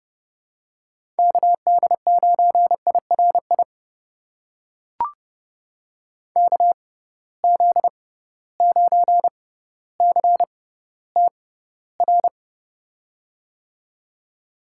Morse code, heard as a keyed tone, about 30 wpm, spaced as for 15 wpm, 700 Hz. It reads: KD9IRI KZ9CTR
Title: USA Call Signs - 6 Characters - 6x Character Spacing - 30wpm